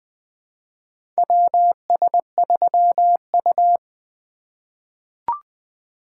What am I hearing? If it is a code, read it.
Morse code, heard as a keyed tone, about 20 wpm, 700 Hz. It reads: WS3U
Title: USA Call Signs - 4 Characters - 4x Character Spacing - 20wpm